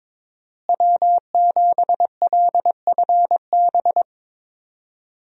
Morse code, heard as a keyed tone, about 22 wpm, 700 Hz. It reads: W7LFB
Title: USA Call Signs - 5 Characters - Repeated 3x - 22wpm